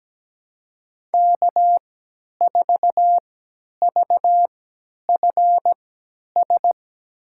Morse code, heard as a keyed tone, about 17 wpm, spaced as for 10 wpm, 700 Hz. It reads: K4VFS
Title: USA Call Signs - 5 Characters - 3x Character Spacing - 17wpm